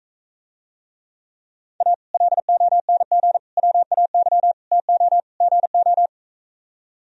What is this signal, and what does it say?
Morse code, heard as a keyed tone, about 45 wpm, 700 Hz. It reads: A LONG WAY TO GO